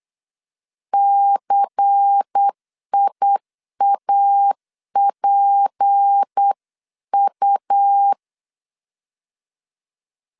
Morse code, heard as a keyed tone, about 8 wpm, 800 Hz. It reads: CIAPU
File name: LEZIONI / 7 GF9Z / 7 GF9Z 8wpm 3213